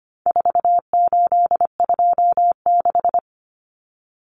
Morse code, heard as a keyed tone, about 25 wpm, 700 Hz. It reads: 4826
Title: Numbers - 4-Digit 25wpm